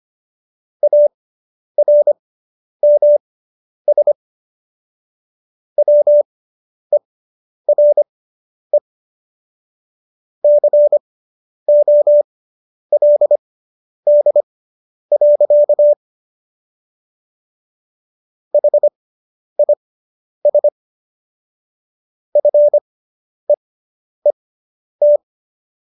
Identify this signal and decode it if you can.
Morse code, heard as a keyed tone, about 25 wpm, spaced as for 10 wpm, 590 Hz. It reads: ARMS WERE COLD. HIS FEET